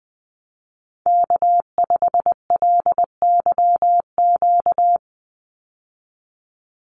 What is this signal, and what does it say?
Morse code, heard as a keyed tone, about 20 wpm, 700 Hz. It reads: K5LYQ
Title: USA Call Signs 20wpm